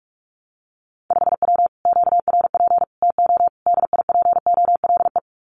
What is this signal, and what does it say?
Morse code, heard as a keyed tone, about 46 wpm, 700 Hz. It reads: 5W QRP TO DIPOLE
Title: Common QSO Phrases Encore 45wpm